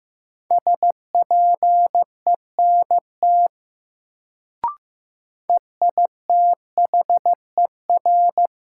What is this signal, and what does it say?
Morse code, heard as a keyed tone, about 15 wpm, 700 Hz. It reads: SPENT EITHER
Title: Sets of 5 Words - New Word Drill - 15wpm